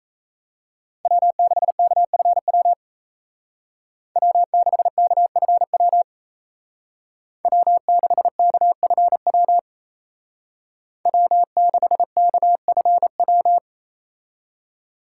Morse code, heard as a keyed tone, about 33 wpm, 700 Hz. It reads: W6KFW W6KFW W6KFW W6KFW
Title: Speed Racing - USA 1x3 Call Signs 28wpm